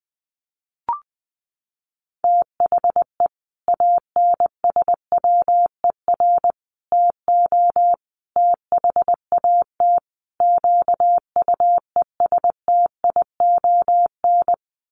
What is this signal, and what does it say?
Morse code, heard as a keyed tone, about 20 wpm, 700 Hz. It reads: THE ANSWER TO THAT QUESTION
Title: Sets of 5 Words Encore2 20wpm